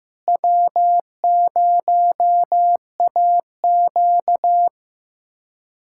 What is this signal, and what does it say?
Morse code, heard as a keyed tone, about 16 wpm, 700 Hz. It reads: W0AQ
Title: USA Callsign DE Callsign - 4 Characters - 15wpm